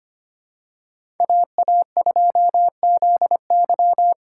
Morse code, heard as a keyed tone, about 25 wpm, 700 Hz. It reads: AA2ZY
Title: USA Call Signs - 5 Characters - 3x Character Spacing - 25wpm